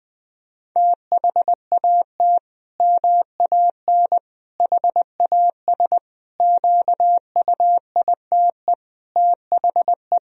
Morse code, heard as a keyed tone, about 21 wpm, 700 Hz. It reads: THAT MAN HAS QUITE THE
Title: Sentences from Top 1000 Words - Review Part 1 - 20wpm